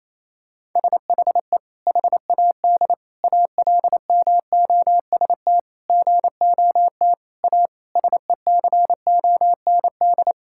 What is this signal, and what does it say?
Morse code, heard as a keyed tone, about 28 wpm, 700 Hz. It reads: SHE HAD ALMOST GOT A SECOND
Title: Sentences from Top 1000 Words - Review Part 5 - 28wpm